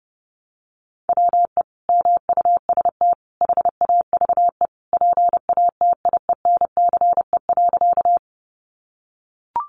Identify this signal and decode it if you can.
Morse code, heard as a keyed tone, about 30 wpm, 700 Hz. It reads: WE MUST HAVE PATIENCE.
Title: Sentences from Top 1300 Words 30wpm